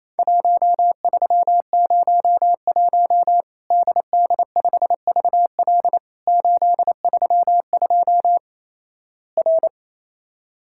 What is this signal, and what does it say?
Morse code, heard as a keyed tone, about 28 wpm, 700 Hz. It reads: 1301 DD5VL 832 R
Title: Worked All Europe DX Contest - QTC Exchange - 28wpm